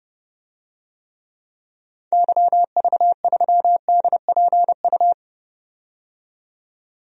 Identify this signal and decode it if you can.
Morse code, heard as a keyed tone, about 32 wpm, 700 Hz. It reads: YV3DPU
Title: DXCC-100 Call Signs - 6 Characters - 30wpm